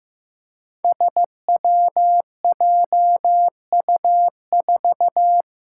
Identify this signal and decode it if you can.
Morse code, heard as a keyed tone, about 15 wpm, 700 Hz. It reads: SWJU4
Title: Five Challenging Letter-Number Combinations 15wpm